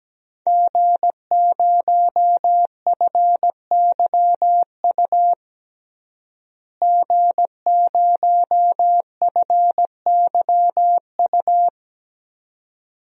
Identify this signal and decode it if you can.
Morse code, heard as a keyed tone, about 17 wpm, 700 Hz. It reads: G0FYU G0FYU
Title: DXCC-100 Call Signs - 5 Characters - Repeated 2x - 17wpm